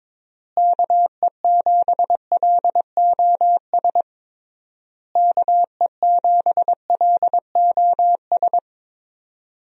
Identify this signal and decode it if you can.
Morse code, heard as a keyed tone, about 22 wpm, 700 Hz. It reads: KE7LOS KE7LOS
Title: USA Call Signs - 6 Characters - Repeated 3x - 22wpm